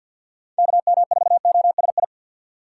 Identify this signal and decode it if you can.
Morse code, heard as a keyed tone, about 48 wpm, 700 Hz. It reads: KM4OSI
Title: USA Call Signs - 6 Characters - 4x Character Spacing - 50wpm